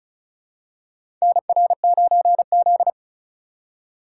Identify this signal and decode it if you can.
Morse code, heard as a keyed tone, about 36 wpm, 700 Hz. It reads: NR9Z
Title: USA Call Signs - 4 Characters - 4x Character Spacing - 35wpm